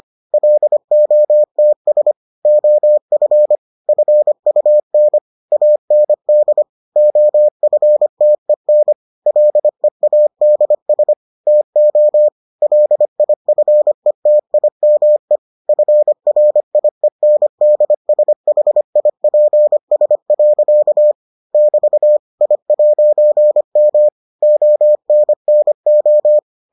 Morse code, heard as a keyed tone, about 25 wpm, 600 Hz. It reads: LOTS OF FUN AND OFTEN LEADS TO LIFETIME FRIENDSHIPS. <BT> I'M ONNO